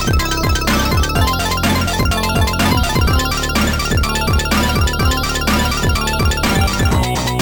There is no code.